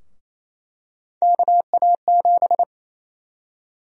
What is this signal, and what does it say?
Morse code, heard as a keyed tone, about 28 wpm, 700 Hz. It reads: KA7
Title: USA Call Sign Prefixes - 28wpm